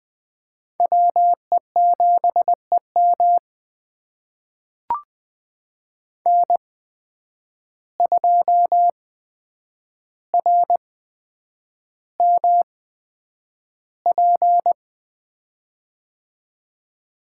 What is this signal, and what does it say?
Morse code, heard as a keyed tone, about 20 wpm, spaced as for 9 wpm, 700 Hz. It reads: WE7EM N2RMP
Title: USA Call Signs - 8x Character Spacing - 20wpm